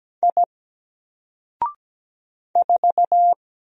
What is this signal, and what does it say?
Morse code, heard as a keyed tone, about 17 wpm, 700 Hz. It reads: I 4
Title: Single Letter-Number 17wpm